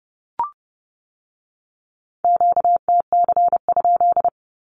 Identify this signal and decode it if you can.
Morse code, heard as a keyed tone, about 30 wpm, 700 Hz. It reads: QTC?
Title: Q-Codes with Spoken Meaning - 30wpm